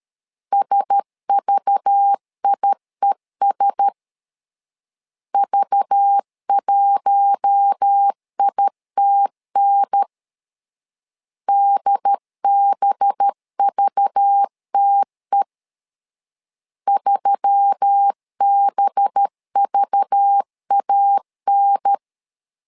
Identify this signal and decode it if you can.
Morse code, heard as a keyed tone, about 13 wpm, 800 Hz. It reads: SVIES V1ITN DBVTE 3BVAN
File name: LEZIONI / 5 LBVC / 5 LBV 12wpm 1855